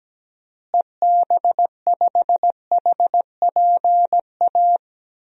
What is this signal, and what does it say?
Morse code, heard as a keyed tone, about 17 wpm, 700 Hz. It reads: EB5HPA
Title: DXCC-100 Call Signs - 6 Characters - Repeated 3x - 17wpm